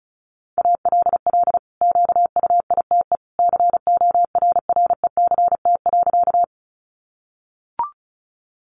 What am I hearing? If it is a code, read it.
Morse code, heard as a keyed tone, about 34 wpm, 700 Hz. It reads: ALL QUITE CORRECT.
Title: Sentences from Top 700 Words 35wpm